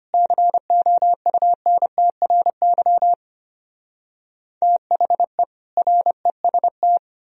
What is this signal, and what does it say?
Morse code, heard as a keyed tone, about 33 wpm, 700 Hz. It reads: COUNTRY THE REST